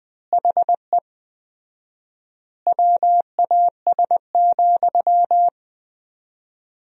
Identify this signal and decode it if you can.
Morse code, heard as a keyed tone, about 20 wpm, 700 Hz. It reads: HE WAS,
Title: 4x Word-Spacing - Sentences from Top 400 Words - 20wpm